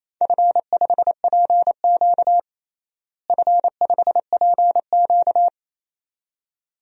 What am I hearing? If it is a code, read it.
Morse code, heard as a keyed tone, about 28 wpm, 700 Hz. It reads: F5PQ F5PQ
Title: DXCC-100 Call Signs - 4 Characters - Repeated 2x - 28wpm